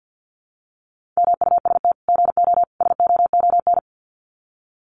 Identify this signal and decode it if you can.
Morse code, heard as a keyed tone, about 50 wpm, 700 Hz. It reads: MUST GO SOON
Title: Common QSO Phrases 50wpm